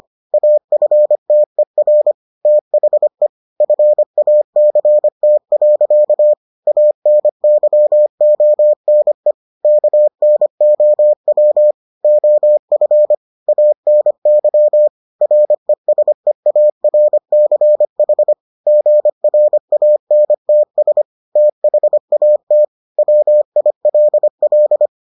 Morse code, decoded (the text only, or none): AFTER THE FACT. ANYONE KNOW OF ANY RESEARCH GRANTS THAT WILL